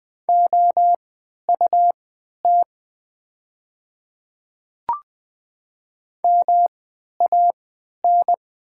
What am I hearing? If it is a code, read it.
Morse code, heard as a keyed tone, about 20 wpm, spaced as for 11 wpm, 690 Hz. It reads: OUT MAN
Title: 3x Character Spacing - Top 200 Words - 20wpm